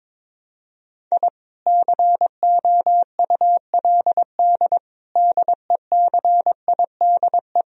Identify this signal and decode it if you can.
Morse code, heard as a keyed tone, about 22 wpm, 700 Hz. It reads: I COULD DECIDE
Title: Sentences from Top 1000 Words - Review Part 3 - 22wpm